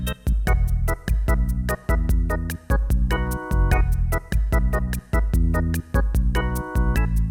There is no code.